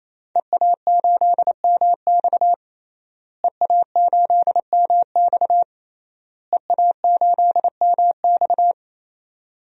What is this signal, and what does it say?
Morse code, heard as a keyed tone, about 28 wpm, 700 Hz. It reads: EA8MX EA8MX EA8MX
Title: DXCC-100 Call Signs - 5 Characters - Repeated 3x - 28wpm